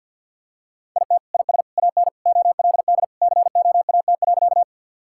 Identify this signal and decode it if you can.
Morse code, heard as a keyed tone, about 50 wpm, 700 Hz. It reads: IT IS AN OLD COAT.
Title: Sentences from Top 1600 Words 50wpm